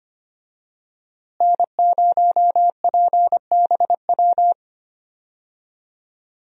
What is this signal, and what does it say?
Morse code, heard as a keyed tone, about 25 wpm, 700 Hz. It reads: N0PBW
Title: USA Call Signs - 5 Characters - Repeated 2x - 25wpm